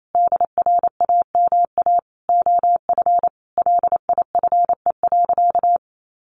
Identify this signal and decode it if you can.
Morse code, heard as a keyed tone, about 28 wpm, 700 Hz. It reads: DRAMA OF LIFE.
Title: Sentences from Top 1500 Words 28wpm